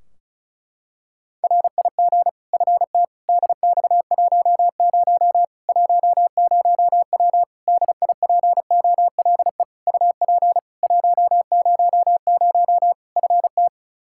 Morse code, heard as a keyed tone, about 36 wpm, 700 Hz. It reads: RIG FT DX10 10W DIPOLE UP 100 FT
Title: Practice QSO - Exchange 2 - Info - 35wpm